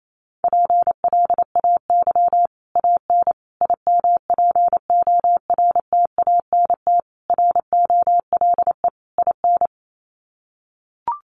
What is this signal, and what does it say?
Morse code, heard as a keyed tone, about 28 wpm, 700 Hz. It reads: PLAY AN IMPORTANT ROLE IN
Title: Sets of 5 Words Encore 28wpm